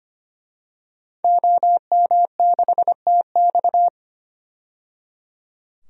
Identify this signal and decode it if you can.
Morse code, heard as a keyed tone, about 25 wpm, 700 Hz. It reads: OM6TX
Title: DXCC-100 Call Signs - 5 Characters - 25wpm